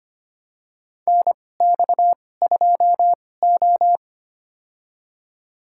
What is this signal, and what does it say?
Morse code, heard as a keyed tone, about 25 wpm, spaced as for 18 wpm, 700 Hz. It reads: NX2O
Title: USA Call Signs - 2x Character Spacing - 25wpm